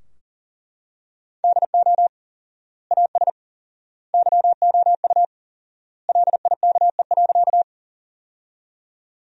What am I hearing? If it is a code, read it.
Morse code, heard as a keyed tone, about 40 wpm, 700 Hz. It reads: DO AS YOU LIKE.
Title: 4x Word-Spacing - Sentences from Top 100 Words - 40wpm